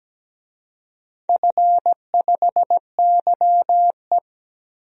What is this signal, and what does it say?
Morse code, heard as a keyed tone, about 17 wpm, 700 Hz. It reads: F5YE
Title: DXCC-100 Call Signs - 4 Characters - Repeated 3x - 17wpm